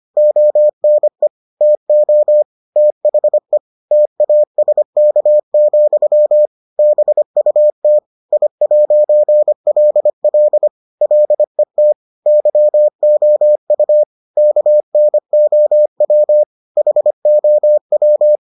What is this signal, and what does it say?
Morse code, heard as a keyed tone, about 25 wpm, 600 Hz. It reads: ONE TO THE TASK, BUT I'LL LET YOU KNOW HOW